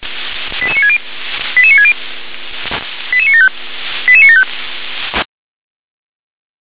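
Five-tone selective call (ZVEI), heard as quick runs of five tones, tones of about 70 ms. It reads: 88960 88960 88975 88975
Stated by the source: ZVEI five-tone selective call received on an SDR.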